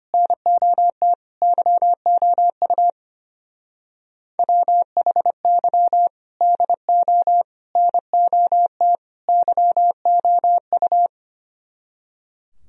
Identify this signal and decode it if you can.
Morse code, heard as a keyed tone, about 31 wpm, 700 Hz. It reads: NOT YOU WHY DO NOT YOU
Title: Speed-Racing - Sets of 4 Words 25wpm